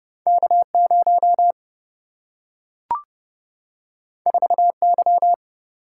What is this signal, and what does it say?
Morse code, heard as a keyed tone, about 30 wpm, 700 Hz. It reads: K0 4Y
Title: Single-Letter Single-Number 30wpm